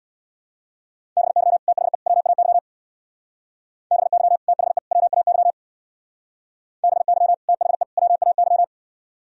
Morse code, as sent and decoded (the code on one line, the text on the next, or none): -... -.-- / - .... / .-- .- -.-- / -... -.-- / - .... / .-- .- -.-- / -... -.-- / - .... / .-- .- -.--
BY THE WAY BY THE WAY BY THE WAY